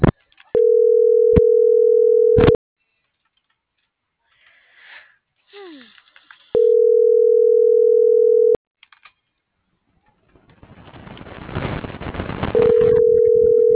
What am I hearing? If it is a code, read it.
Ringback tone (North America), 440 + 480 Hz.